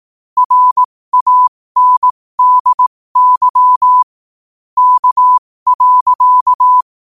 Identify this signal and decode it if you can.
Morse code, heard as a keyed tone, about 18 wpm, spaced as for 15 wpm, 1000 Hz. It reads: RANDY K.
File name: qotdfiles / 15wpm